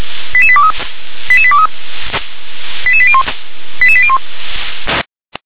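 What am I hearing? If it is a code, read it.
Five-tone selective call (ZVEI), heard as quick runs of five tones, tones of about 70 ms. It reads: 88823 88823 88801 88801
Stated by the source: ZVEI five-tone selective call 88085 received via SDR.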